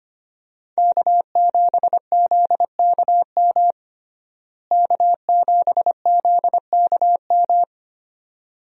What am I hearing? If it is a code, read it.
Morse code, heard as a keyed tone, about 25 wpm, 700 Hz. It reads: K7ZKM K7ZKM
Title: USA Call Signs - 5 Characters - Repeated 2x - 25wpm